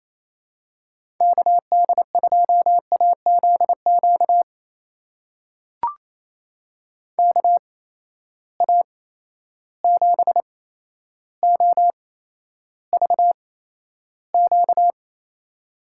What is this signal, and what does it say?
Morse code, heard as a keyed tone, about 28 wpm, spaced as for 12 wpm, 700 Hz. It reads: KD2AZQ KA7OVQ
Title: USA Call Signs - 8x Character Spacing - 28wpm